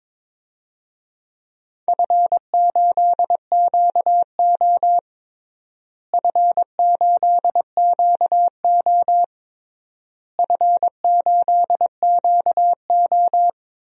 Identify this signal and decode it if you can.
Morse code, heard as a keyed tone, about 22 wpm, 700 Hz. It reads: F8QO F8QO F8QO
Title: DXCC-100 Call Signs - 4 Characters - Repeated 3x - 22wpm